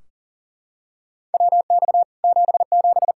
Morse code, heard as a keyed tone, about 40 wpm, 700 Hz. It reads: WX 77
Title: Practice QSO - Exchange 2 - Info - 40wpm